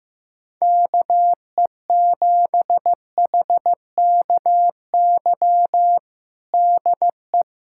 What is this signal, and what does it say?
Morse code, heard as a keyed tone, about 15 wpm, 700 Hz. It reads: KE7HKY DE
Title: USA Callsign DE Callsign - 6 Characters - 15wpm